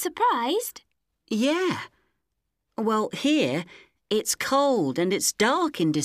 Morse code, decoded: none